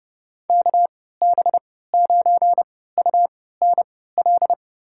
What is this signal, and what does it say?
Morse code, heard as a keyed tone, about 32 wpm, spaced as for 17 wpm, 700 Hz. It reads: KB9UNL
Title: USA Call Signs - 6 Characters - 3x Character Spacing - 30wpm